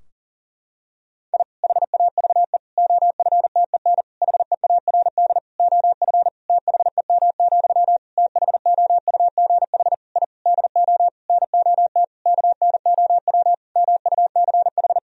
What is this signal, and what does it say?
Morse code, heard as a keyed tone, about 40 wpm, 700 Hz. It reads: I HAVE OFTEN HEARD OF THEM, THOUGH I DO NOT KNOW MUCH
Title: Sentences from Top 1000 Words - Review Part 2 - 40wpm